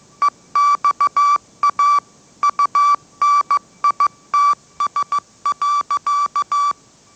Morse code, heard as a keyed tone, about 15 wpm, 1160 Hz. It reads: EXA UNITS.
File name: mrx / mrx066 16wpm